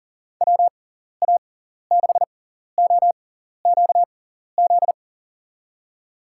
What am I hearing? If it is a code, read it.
Morse code, heard as a keyed tone, about 40 wpm, spaced as for 14 wpm, 700 Hz. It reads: WA6OQZ
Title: USA Call Signs - 6x Character Spacing - 40wpm